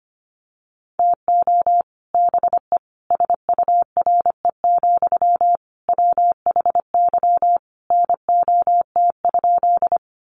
Morse code, heard as a keyed tone, about 25 wpm, 700 Hz. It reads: TO BE SURE, WHY NOT?
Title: Sentences from Top 600 Words 25wpm